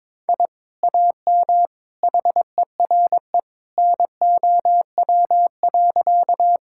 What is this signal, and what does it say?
Morse code, heard as a keyed tone, about 22 wpm, 700 Hz. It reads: I AM HERE NOW.
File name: practice / 22wpm / Sentences with 2-Word Phrases - 22wpm